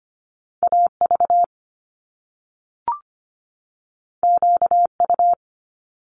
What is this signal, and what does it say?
Morse code, heard as a keyed tone, about 25 wpm, 700 Hz. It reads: AV QU
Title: Top Two Letter Combinations in Words 25wpm